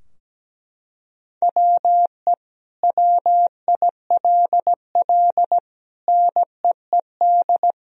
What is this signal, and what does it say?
Morse code, heard as a keyed tone, about 17 wpm, 700 Hz. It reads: WE WILL NEED
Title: Sentences from Top 1000 Words - Review Part 4 - 17wpm